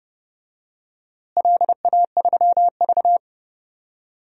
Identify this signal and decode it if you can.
Morse code, heard as a keyed tone, about 32 wpm, 700 Hz. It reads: LA3V
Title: DXCC-100 Call Signs - 4 Characters - Repeated 2x - 30wpm